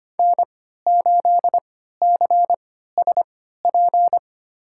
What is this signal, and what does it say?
Morse code, heard as a keyed tone, about 25 wpm, spaced as for 14 wpm, 700 Hz. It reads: N8CSP
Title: USA Call Signs - 3x Character Spacing - 25wpm